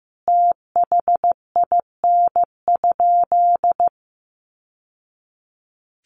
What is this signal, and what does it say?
Morse code, heard as a keyed tone, about 15 wpm, 700 Hz. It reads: THIN?
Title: Sentences from Top 1200 Words 15wpm